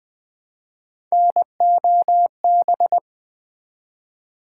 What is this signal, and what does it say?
Morse code, heard as a keyed tone, about 20 wpm, 700 Hz. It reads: NOB
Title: Build A Word - 1 Letter At A Time - 5-Letter Words - 20wpm